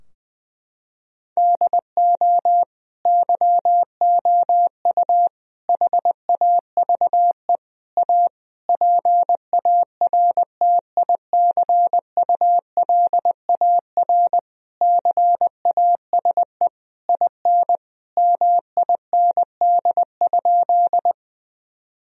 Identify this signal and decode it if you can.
Morse code, heard as a keyed tone, about 20 wpm, 700 Hz. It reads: DO YOU HAVE A PARTICULAR CASE IN MIND?
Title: Sentences from Top 1000 Words - Review Part 4 - 20wpm